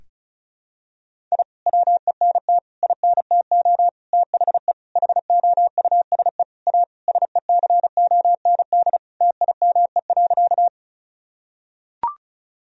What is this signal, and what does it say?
Morse code, heard as a keyed tone, about 36 wpm, 700 Hz. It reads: I WENT INTO THE HOUSE A SECOND TIME.
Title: Sentences from Top 1000 Words - Review Part 1 - 35wpm